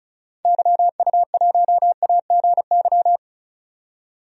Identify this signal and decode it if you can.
Morse code, heard as a keyed tone, about 36 wpm, 690 Hz. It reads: YU1AGY
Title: DXCC-100 Call Signs - 6 Characters - 35wpm